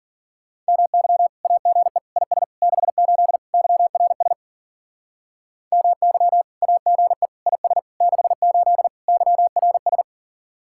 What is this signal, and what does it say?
Morse code, heard as a keyed tone, about 49 wpm, 700 Hz. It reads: MY AGE IS 68 YRS MY AGE IS 68 YRS
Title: Speed-Racing Common QSO Phrases Encore - 40wpm